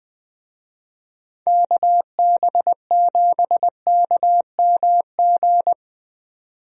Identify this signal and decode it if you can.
Morse code, heard as a keyed tone, about 20 wpm, 700 Hz. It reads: KB7KMG
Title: USA Call Signs - 6 Characters - 20wpm